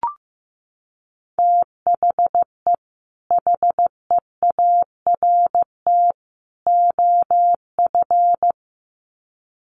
Morse code, heard as a keyed tone, about 15 wpm, 700 Hz. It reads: THE HEART OF
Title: Sets of 3 Words 15wpm